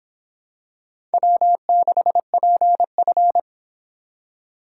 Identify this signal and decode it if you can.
Morse code, heard as a keyed tone, about 25 wpm, 700 Hz. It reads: W6PF